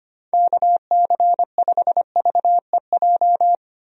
Morse code, heard as a keyed tone, about 25 wpm, 700 Hz. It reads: KC5VEJ
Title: USA Call Signs - 6 Characters - 25wpm